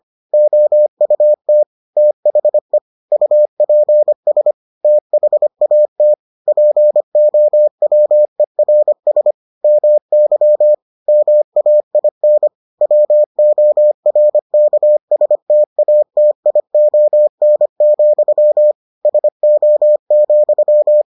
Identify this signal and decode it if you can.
Morse code, heard as a keyed tone, about 25 wpm, 600 Hz. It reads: OUT THE UPS THAT POWERS MY MAIN WORKSTATION, SO,